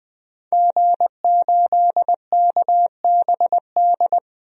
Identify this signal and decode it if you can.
Morse code, heard as a keyed tone, about 20 wpm, 700 Hz. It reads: G8KBD
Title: DXCC-100 Call Signs - 5 Characters - Repeated 2x - 20wpm